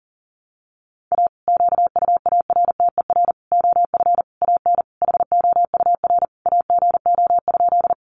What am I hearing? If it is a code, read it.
Morse code, heard as a keyed tone, about 40 wpm, 700 Hz. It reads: A QUARTER OF AN HOUR AGO?
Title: Sentences from Top 1300 Words 40wpm